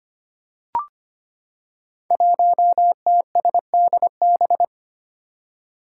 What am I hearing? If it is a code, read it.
Morse code, heard as a keyed tone, about 25 wpm, 700 Hz. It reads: 1TSDB
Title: Five Challenging Letter-Number Combinations 25wpm